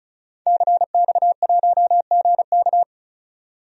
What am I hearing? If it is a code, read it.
Morse code, heard as a keyed tone, about 36 wpm, 700 Hz. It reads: CX1GK
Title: DXCC-100 Call Signs - 5 Characters - 35wpm